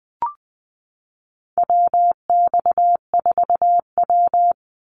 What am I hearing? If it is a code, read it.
Morse code, heard as a keyed tone, about 20 wpm, 700 Hz. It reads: WX4W
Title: Call Signs - 20wpm